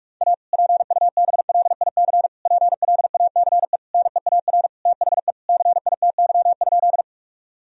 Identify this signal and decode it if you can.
Morse code, heard as a keyed tone, about 46 wpm, 700 Hz. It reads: A PUBLIC PLACE NEAR THE CITY?
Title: Sentences from Top 1000 Words - Review Part 3 - 45wpm